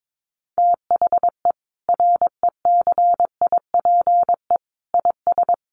Morse code, heard as a keyed tone, about 22 wpm, 700 Hz. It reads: THE RECIPE IS
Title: Sentences from Top 1200 Words 22wpm